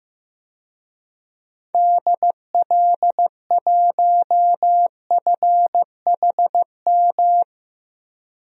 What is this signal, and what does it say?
Morse code, heard as a keyed tone, about 15 wpm, 700 Hz. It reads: DL1FHM
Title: DXCC-100 Call Signs - 6 Characters - 15wpm